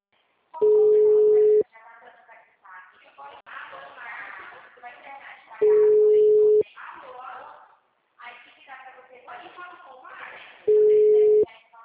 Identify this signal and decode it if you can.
Ringback tone (Europe, 425 Hz), with speech.